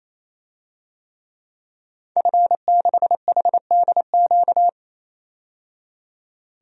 Morse code, heard as a keyed tone, about 28 wpm, 700 Hz. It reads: F6HDQ
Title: DXCC-100 Call Signs - 5 Characters - 28wpm